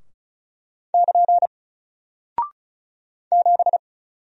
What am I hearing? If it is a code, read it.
Morse code, heard as a keyed tone, about 36 wpm, 700 Hz. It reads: <KN> 7
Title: Single Character (Letter:Number:Punctuation:Prosign) Alt 35wpm